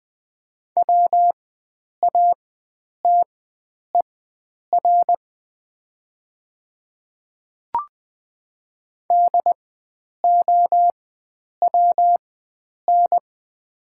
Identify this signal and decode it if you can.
Morse code, heard as a keyed tone, about 20 wpm, spaced as for 9 wpm, 700 Hz. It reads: WATER DOWN
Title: 4x Character Spacing - Top 100 Words - 20wpm